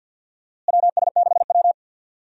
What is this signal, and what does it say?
Morse code, heard as a keyed tone, about 52 wpm, 700 Hz. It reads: WS6W
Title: USA Call Signs - 4 Characters - Repeated 3x - 50wpm